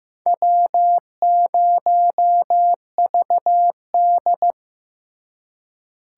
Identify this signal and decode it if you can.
Morse code, heard as a keyed tone, about 18 wpm, 700 Hz. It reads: W0VD
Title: Speed Racing - USA 1x2 Call Signs 15wpm